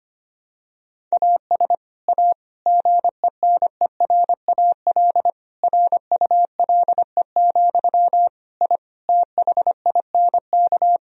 Morse code, heard as a keyed tone, about 25 wpm, 700 Hz. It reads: AS A GENERAL RULE, I THINK
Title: Sentences from Top 1000 Words - Review Part 3 - 25wpm